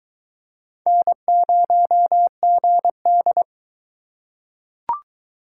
Morse code, heard as a keyed tone, about 22 wpm, 700 Hz. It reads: N0GD